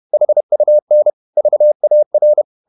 Morse code, heard as a keyed tone, about 31 wpm, 600 Hz. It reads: HUN VAR
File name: Ferie QRQ 31wpm